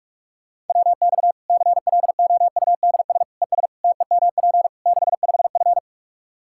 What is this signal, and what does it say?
Morse code, heard as a keyed tone, about 41 wpm, 700 Hz. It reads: WX CLOUDS ES TEMP 65F
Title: Practice QSO - Exchange 2 - Info - 45wpm